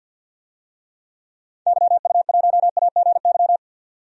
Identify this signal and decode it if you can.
Morse code, heard as a keyed tone, about 52 wpm, 700 Hz. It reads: YU1AGY
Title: DXCC-100 Call Signs - 6 Characters - 50wpm